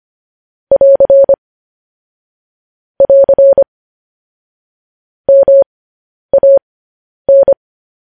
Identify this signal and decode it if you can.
Morse code, heard as a keyed tone, about 25 wpm, spaced as for 10 wpm, 550 Hz. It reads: <AR> <AR> MAN
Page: News Headlines in Morse Code at 10 WPM